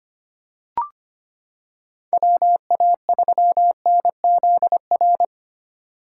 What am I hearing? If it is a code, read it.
Morse code, heard as a keyed tone, about 25 wpm, 700 Hz. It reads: WA3NZR
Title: K1USN SST QSO Practice - 25wpm